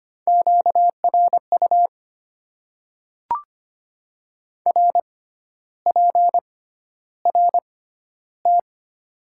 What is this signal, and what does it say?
Morse code, heard as a keyed tone, about 25 wpm, spaced as for 12 wpm, 690 Hz. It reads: QRU RPRT
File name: Top 90 QSO Elements - 6x Character Spacing - 25wpm